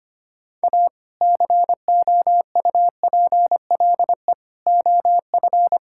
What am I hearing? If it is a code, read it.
Morse code, heard as a keyed tone, about 25 wpm, 700 Hz. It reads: A COUPLE OF
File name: practice / 25wpm / Sentences with 2-Word Phrases - 25wpm